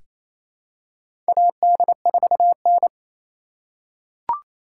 Morse code, heard as a keyed tone, about 28 wpm, 700 Hz. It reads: AD4N
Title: USA Call Signs - 4 Characters - 3x Character Spacing - 28wpm